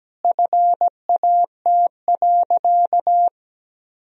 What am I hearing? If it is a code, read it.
Morse code, heard as a keyed tone, about 17 wpm, 700 Hz. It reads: FAT.
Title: Sentences from Top 1000 Words - Review Part 4 - 17wpm